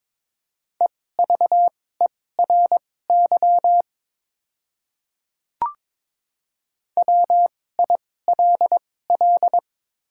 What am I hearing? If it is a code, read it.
Morse code, heard as a keyed tone, about 22 wpm, spaced as for 16 wpm, 700 Hz. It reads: EVERY WILL
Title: 2x Character Spacing - Top 200 Words - 22wpm